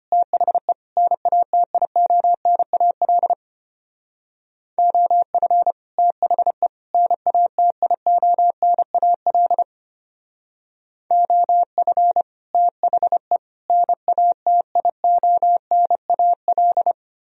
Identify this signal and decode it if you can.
Morse code, heard as a keyed tone, about 32 wpm, 700 Hz. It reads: THE NATIONAL OF THE NATIONAL OF THE NATIONAL